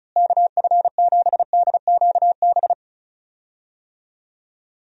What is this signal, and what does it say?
Morse code, heard as a keyed tone, about 36 wpm, 700 Hz. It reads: KF7DQB
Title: USA Call Signs - 6 Characters - Repeated 2x - 35wpm